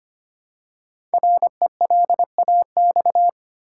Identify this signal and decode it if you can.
Morse code, heard as a keyed tone, about 25 wpm, 700 Hz. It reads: RELAX
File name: 5 Letter Words - First 3 Letters Given - 25wpm